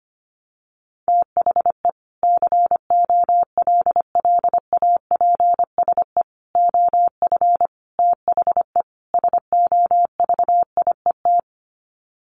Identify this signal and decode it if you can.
Morse code, heard as a keyed tone, about 25 wpm, 700 Hz. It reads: THE COLLAPSE OF THE SOVIET
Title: Sets of 5 Words Encore 25wpm